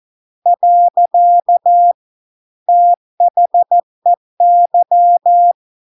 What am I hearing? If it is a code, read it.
Morse code, heard as a keyed tone, about 14 wpm, 700 Hz. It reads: . THEY